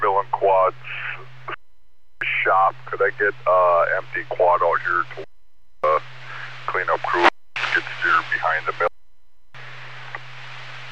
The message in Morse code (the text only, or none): none